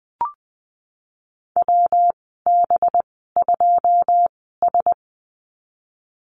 Morse code, heard as a keyed tone, about 20 wpm, spaced as for 14 wpm, 700 Hz. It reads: WB2S
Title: 193 Lesson 31g - Letter v - Call Signs 20wpm